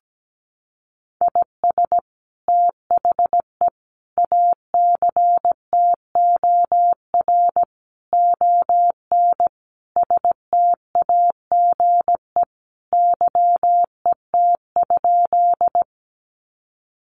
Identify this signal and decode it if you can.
Morse code, heard as a keyed tone, about 17 wpm, 700 Hz. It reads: IS THE ACTOR ON STAGE YET?
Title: Sentences from Top 1500 Words 17wpm